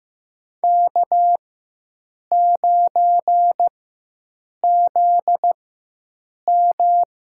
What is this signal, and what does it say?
Morse code, heard as a keyed tone, about 16 wpm, spaced as for 7 wpm, 700 Hz. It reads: K9ZM
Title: USA Call Signs - 4x Character Spacing - 15wpm